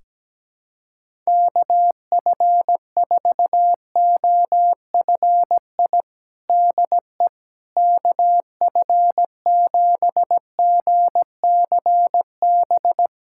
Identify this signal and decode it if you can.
Morse code, heard as a keyed tone, about 17 wpm, 700 Hz. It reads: KF4OFI DE KF7GCB
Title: USA Callsign DE Callsign - 6 Characters - 17wpm